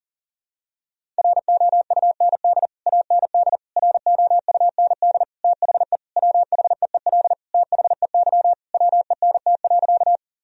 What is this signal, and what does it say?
Morse code, heard as a keyed tone, about 40 wpm, 700 Hz. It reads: ROUND AND ROUND THE WHEEL THEY WENT.
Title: Sentences from Top 1000 Words - Review Part 3 - 40wpm